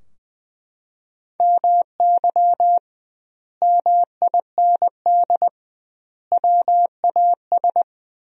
Morse code, heard as a keyed tone, about 20 wpm, 700 Hz. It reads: MY MIND WAS
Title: 2x Word-Spacing - Sentences from Top 500 Words - 20wpm